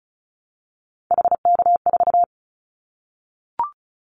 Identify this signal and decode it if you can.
Morse code, heard as a keyed tone, about 36 wpm, 700 Hz. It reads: HK4
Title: Callsign Prefixes for Top 100 DXCC - 35wpm